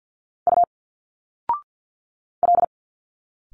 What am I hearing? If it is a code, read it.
Morse code, heard as a keyed tone, about 50 wpm, 700 Hz. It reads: U L
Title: Single Letters 50wpm